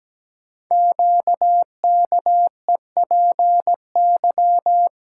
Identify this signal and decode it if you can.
Morse code, heard as a keyed tone, about 17 wpm, 700 Hz. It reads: QKEPY
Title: Code Groups - 5 Letters - 17wpm